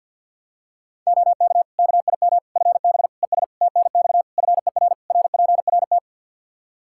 Morse code, heard as a keyed tone, about 47 wpm, 700 Hz. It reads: OK KIM FB ES TNX FER RPRT
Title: Practice QSO - Exchange 2 - Info - 50wpm